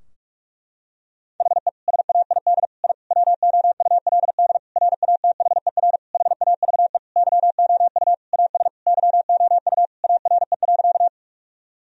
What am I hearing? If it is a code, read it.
Morse code, heard as a keyed tone, about 46 wpm, 700 Hz. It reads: HE SAID I WOULD RATHER HAVE YOU AS YOU ARE.
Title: Sentences from Top 1000 Words - Review Part 4 - 45wpm